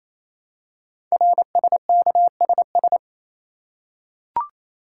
Morse code, heard as a keyed tone, about 28 wpm, 700 Hz. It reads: RSKSS
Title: Code Groups - 5 Letters - 28wpm